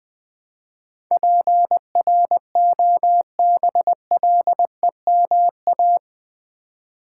Morse code, heard as a keyed tone, about 19 wpm, 700 Hz. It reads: PROBLEMA
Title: Build A Word - 4 Letters At A Time - 11-Letter Words - 20wpm